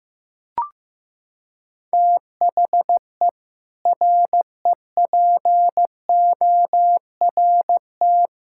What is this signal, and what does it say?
Morse code, heard as a keyed tone, about 16 wpm, 700 Hz. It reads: THE REPORT
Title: Sentences from Top 1000 Words - Review Part 6 - 15wpm